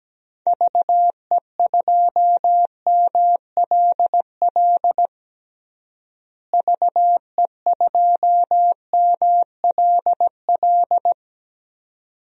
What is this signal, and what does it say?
Morse code, heard as a keyed tone, about 17 wpm, 700 Hz. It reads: VE2MLL VE2MLL
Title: DXCC-100 Call Signs - 6 Characters - Repeated 2x - 17wpm